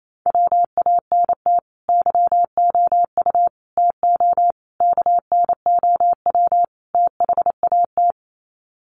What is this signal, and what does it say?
Morse code, heard as a keyed tone, about 28 wpm, 700 Hz. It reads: WANT YOU TO KNOW THAT
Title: Sets of 5 Words Encore 28wpm